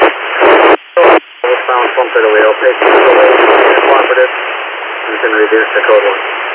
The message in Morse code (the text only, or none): none